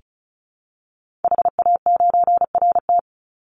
Morse code, heard as a keyed tone, about 34 wpm, 700 Hz. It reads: HA9RT
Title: Call Signs Repeated 2x - 35wpm